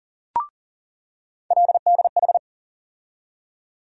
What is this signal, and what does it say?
Morse code, heard as a keyed tone, about 40 wpm, 700 Hz. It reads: LDH
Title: USA Call Sign Postfixes - 3 Letters 40wpm